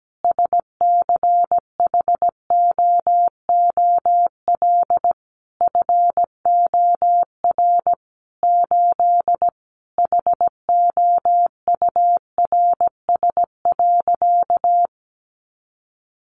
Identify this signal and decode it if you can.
Morse code, heard as a keyed tone, about 17 wpm, 700 Hz. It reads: SCHOOL FOR 8 HOURS.
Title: Additional Easy Phrases 17wpm